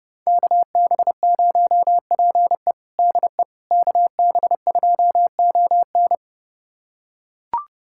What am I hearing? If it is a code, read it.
Morse code, heard as a keyed tone, about 30 wpm, 700 Hz. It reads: KB0PE DE KB2ON
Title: USA Callsign DE Callsign - 5 Characters - 30wpm